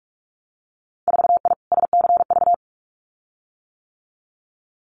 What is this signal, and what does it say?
Morse code, heard as a keyed tone, about 46 wpm, 700 Hz. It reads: 4I SCV
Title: Winter Field Day Exchanges Encore 45wpm